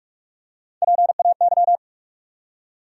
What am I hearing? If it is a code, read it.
Morse code, heard as a keyed tone, about 41 wpm, 700 Hz. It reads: PAY
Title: Three Letter Words - 45wpm